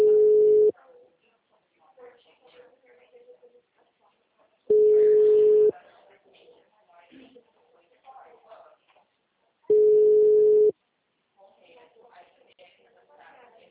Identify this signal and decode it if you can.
Ringback tone (Europe, 425 Hz), with speech.
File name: out- -2025- - - .43571